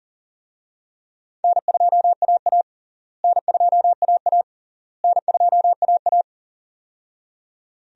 Morse code, heard as a keyed tone, about 40 wpm, 700 Hz. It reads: N2AA N2AA N2AA
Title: USA Call Signs - 4 Characters - Repeated 3x - 40wpm